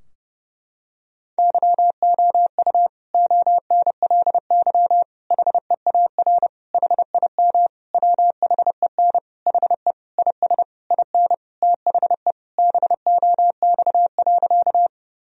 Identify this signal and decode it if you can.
Morse code, heard as a keyed tone, about 32 wpm, 700 Hz. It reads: YOU ONLY HEAR HIM WHEN HE IS IN THE BOX.
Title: Sentences from Top 1000 Words - Review Part 2 - 30wpm